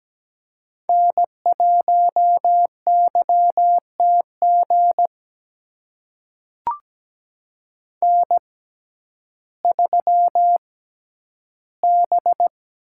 Morse code, heard as a keyed tone, about 17 wpm, spaced as for 10 wpm, 700 Hz. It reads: N1YTG N3B
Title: USA Call Signs - 5 Characters - 6x Character Spacing - 17wpm